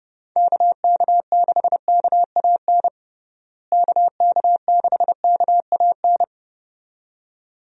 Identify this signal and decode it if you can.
Morse code, heard as a keyed tone, about 32 wpm, 700 Hz. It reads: KK6KAN KK6KAN
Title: USA Call Signs - 6 Characters - Repeated 2x - 30wpm